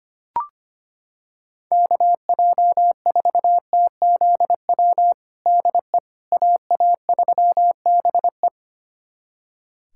Morse code, heard as a keyed tone, about 25 wpm, 700 Hz. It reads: KJ4TZW DE AA3BE
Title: USA Callsign DE Callsign 25wpm